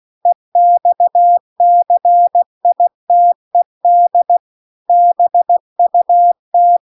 Morse code, heard as a keyed tone, about 16 wpm, 700 Hz. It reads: EXCITED BUT